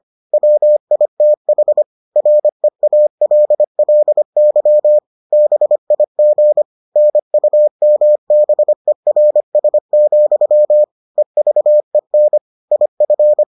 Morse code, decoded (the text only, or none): WITH REALLY BIG NUMBERS, EVEN IF